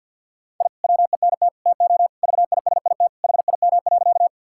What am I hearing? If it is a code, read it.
Morse code, heard as a keyed tone, about 50 wpm, 700 Hz. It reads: I WENT TO VISIT HIM.
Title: Sentences from Top 1000 Words - Review Part 6 - 50wpm